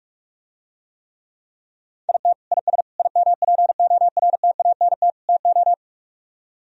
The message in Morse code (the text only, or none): IT IS IMPORTANT TO